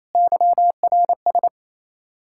Morse code, heard as a keyed tone, about 37 wpm, 700 Hz. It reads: YRS